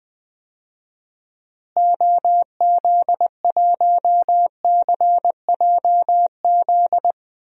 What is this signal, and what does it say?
Morse code, heard as a keyed tone, about 20 wpm, 700 Hz. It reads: OZ1CJZ
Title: DXCC-100 Call Signs - 6 Characters - Repeated 3x - 20wpm